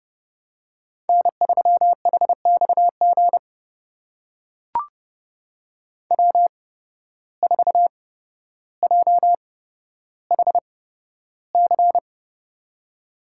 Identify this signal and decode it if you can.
Morse code, heard as a keyed tone, about 30 wpm, spaced as for 13 wpm, 700 Hz. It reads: N3HXG W4JHC
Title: USA Call Signs - 5 Characters - 8x Character Spacing - 30wpm